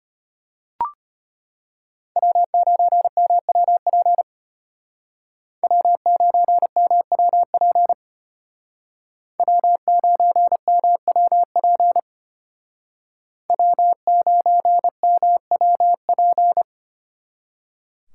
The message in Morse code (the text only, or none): W9MWP W9MWP W9MWP W9MWP